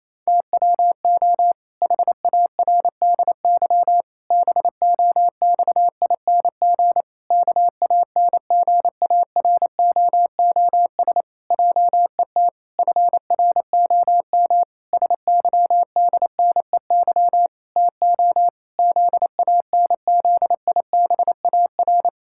Morse code, heard as a keyed tone, about 28 wpm, 700 Hz. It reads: TWO HARDY BOXING KANGAROOS JET FROM SYDNEY TO ZANZIBAR
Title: Pangram Challenge v2 - 28wpm